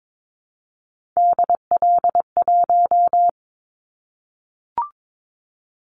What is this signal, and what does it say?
Morse code, heard as a keyed tone, about 22 wpm, 700 Hz. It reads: DL1
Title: Callsign Prefixes for Top 50 DXCC - 22wpm